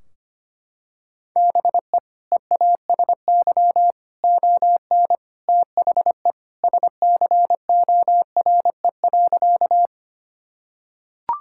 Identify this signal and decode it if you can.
Morse code, heard as a keyed tone, about 25 wpm, 700 Hz. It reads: BE EASY ON THE SCORE.
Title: Sentences from Top 1600 Words 25wpm